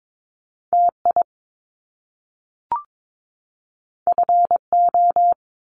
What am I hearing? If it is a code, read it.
Morse code, heard as a keyed tone, about 22 wpm, 700 Hz. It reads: TI FO
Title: Top Two Letter Combinations in Words 22wpm